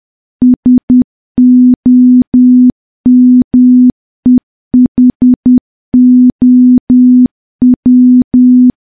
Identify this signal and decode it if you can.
Morse code, heard as a keyed tone, about 10 wpm, 250 Hz. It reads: SOMEHOW
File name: morse / morse-training-10WPM